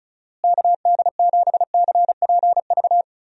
Morse code, heard as a keyed tone, about 36 wpm, 700 Hz. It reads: KD7CPV
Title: USA Call Signs - 6 Characters - Repeated 2x - 35wpm